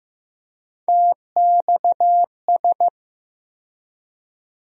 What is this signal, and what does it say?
Morse code, heard as a keyed tone, about 15 wpm, 700 Hz. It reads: TXS
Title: USA Call Sign Postfixes - 3 Letters 15wpm